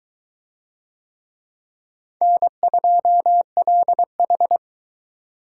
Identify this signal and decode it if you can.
Morse code, heard as a keyed tone, about 22 wpm, 700 Hz. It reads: N2LH